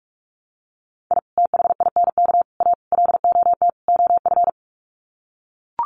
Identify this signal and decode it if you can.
Morse code, heard as a keyed tone, about 44 wpm, 700 Hz. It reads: I THINK A LOT OF
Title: Sets of 5 Words 45wpm